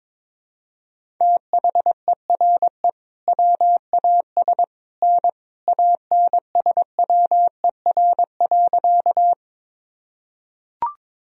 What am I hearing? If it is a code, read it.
Morse code, heard as a keyed tone, about 22 wpm, 700 Hz. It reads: THERE WAS N ANSWER.
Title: Sentences from Top 300 Words and Abbreviations 22wpm